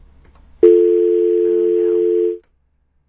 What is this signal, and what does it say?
Dial tone (North America), 350 + 440 Hz.